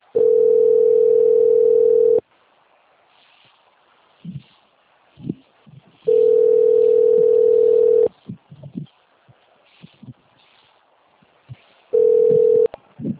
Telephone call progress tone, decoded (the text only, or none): ringback tone